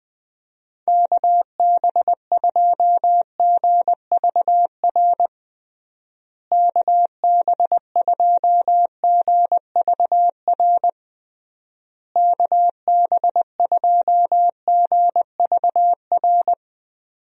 Morse code, heard as a keyed tone, about 21 wpm, 700 Hz. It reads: KB2GVR KB2GVR KB2GVR
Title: USA Call Signs - 6 Characters - Repeated 3x - 20wpm